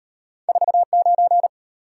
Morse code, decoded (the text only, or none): 49